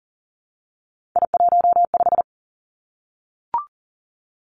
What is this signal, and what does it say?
Morse code, heard as a keyed tone, about 38 wpm, 700 Hz. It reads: I15
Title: Three Challenging Letter-Number Combinations 40wpm